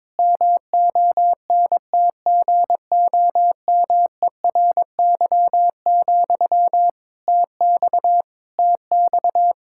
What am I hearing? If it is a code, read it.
Morse code, heard as a keyed tone, about 22 wpm, 700 Hz. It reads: MONTGOMERY, TX TX
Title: K1USN SST QSO Practice 22wpm